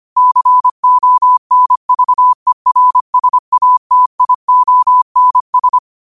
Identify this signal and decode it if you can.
Morse code, heard as a keyed tone, about 25 wpm, 1000 Hz. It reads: CONVERSATIONS